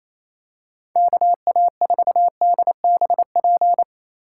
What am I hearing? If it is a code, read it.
Morse code, heard as a keyed tone, about 28 wpm, 700 Hz. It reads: KA4DBP
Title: USA Call Signs - 6 Characters - 3x Character Spacing - 28wpm